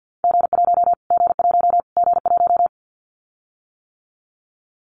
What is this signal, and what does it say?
Morse code, heard as a keyed tone, about 50 wpm, 700 Hz. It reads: G1 G1 G1
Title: Callsign Prefixes for Top 25 DXCC - Repeated 3x - 50wpm